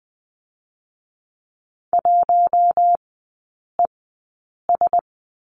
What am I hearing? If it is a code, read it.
Morse code, heard as a keyed tone, about 20 wpm, spaced as for 8 wpm, 700 Hz. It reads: 1ES
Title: 053 Lesson 10e - Letter r - Three Characters 20wpm